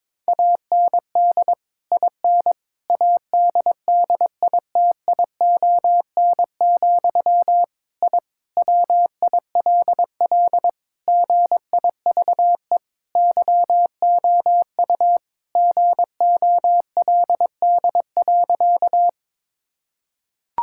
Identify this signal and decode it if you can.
Morse code, heard as a keyed tone, about 22 wpm, 700 Hz. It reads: AND IN ADDITION, I WILL GIVE YOU GOLD.
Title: Sentences from Top 1000 Words - Review Part 6 - 22wpm